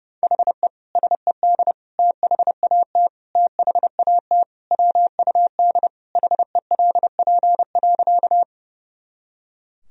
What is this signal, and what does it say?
Morse code, heard as a keyed tone, about 32 wpm, 700 Hz. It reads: HE SED THAT THAT WUD HELP.
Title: Sentences from Top 200 Words and Abbreviations 30wpm